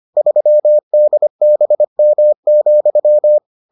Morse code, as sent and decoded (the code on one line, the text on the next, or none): ...-- -.. -... -- --..--
3DBM,